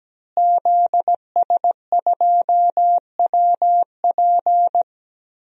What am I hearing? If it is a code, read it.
Morse code, heard as a keyed tone, about 17 wpm, 700 Hz. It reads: ZS2WP
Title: DXCC-100 Call Signs - 5 Characters - Repeated 3x - 17wpm